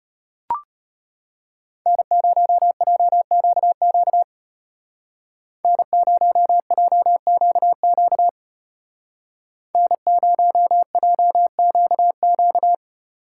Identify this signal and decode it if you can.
Morse code, heard as a keyed tone, about 31 wpm, 700 Hz. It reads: N0JQQ N0JQQ N0JQQ
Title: Speed Racing - USA 1x3 Call Signs 25wpm